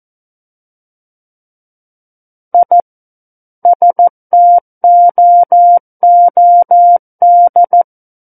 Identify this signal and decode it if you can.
Morse code, heard as a keyed tone, about 14 wpm, 700 Hz. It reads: I STOOD
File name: Book One / 05-Chapter 5 - 14 WPM